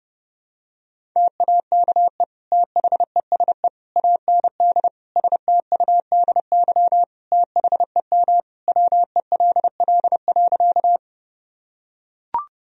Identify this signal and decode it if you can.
Morse code, heard as a keyed tone, about 32 wpm, 700 Hz. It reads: TAKE THESE AND STUDY THEM WELL.
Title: Sentences from Top 1000 Words - Review Part 1 - 30wpm